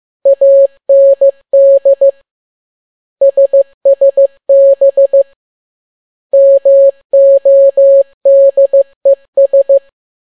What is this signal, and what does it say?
Morse code, heard as a keyed tone, about 15 wpm, 550 Hz. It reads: AND SSB MODES
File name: n6tt p1 15C13